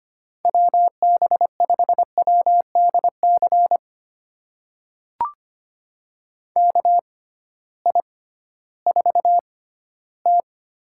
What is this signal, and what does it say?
Morse code, heard as a keyed tone, about 25 wpm, spaced as for 15 wpm, 700 Hz. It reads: WB5WDC KI4T
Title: USA Call Signs - 6 Characters - 6x Character Spacing - 25wpm